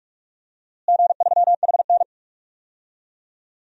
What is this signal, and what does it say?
Morse code, heard as a keyed tone, about 43 wpm, 700 Hz. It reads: G3HN
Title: DXCC-100 Call Signs - 4 Characters - Repeated 3x - 45wpm